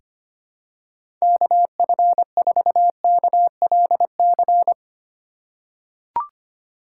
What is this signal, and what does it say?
Morse code, heard as a keyed tone, about 25 wpm, 700 Hz. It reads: KF4KLC
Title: USA Call Signs - 6 Characters - 2x Character Spacing - 25wpm